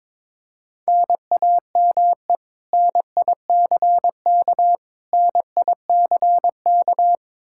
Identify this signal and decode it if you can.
Morse code, heard as a keyed tone, about 22 wpm, 700 Hz. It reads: NAME NICK NICK
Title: Practice QSO - Exchange 1 - Beginning - 22wpm